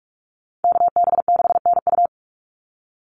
Morse code, heard as a keyed tone, about 44 wpm, 700 Hz. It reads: KB6NU
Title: Call Signs Repeated 2x - 45wpm